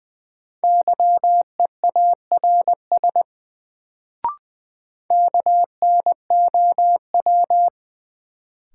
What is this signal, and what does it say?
Morse code, heard as a keyed tone, about 20 wpm, 700 Hz. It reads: YEARS KNOW
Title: Sets of 2 Words - New Word Drill - 20wpm